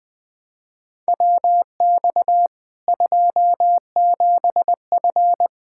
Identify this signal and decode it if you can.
Morse code, heard as a keyed tone, about 21 wpm, 700 Hz. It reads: WX 27F
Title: Practice QSO - Exchange 2 - Info - 20wpm